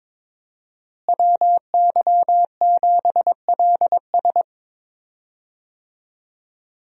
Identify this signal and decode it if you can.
Morse code, heard as a keyed tone, about 22 wpm, 700 Hz. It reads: WY7LS
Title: USA Call Signs - 5 Characters - Repeated 2x - 22wpm